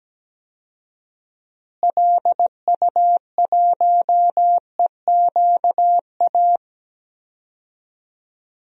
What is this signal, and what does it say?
Morse code, heard as a keyed tone, about 17 wpm, 700 Hz. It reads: LU1EQA
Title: DXCC-100 Call Signs - 6 Characters - Repeated 3x - 17wpm